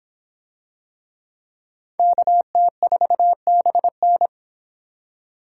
Morse code, heard as a keyed tone, about 26 wpm, 700 Hz. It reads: KT4BN